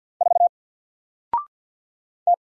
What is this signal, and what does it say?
Morse code, heard as a keyed tone, about 48 wpm, 700 Hz. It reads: 4 T
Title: Single Letter-Number 50wpm